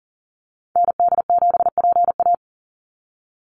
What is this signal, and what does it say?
Morse code, heard as a keyed tone, about 40 wpm, 700 Hz. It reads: ND7PA
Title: NA SOTA Activator Callsigns 40wpm